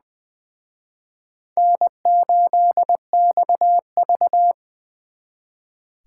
Morse code, heard as a keyed tone, about 24 wpm, 700 Hz. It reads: N8XV
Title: Speed Racing - USA 1x2 Call Signs 20wpm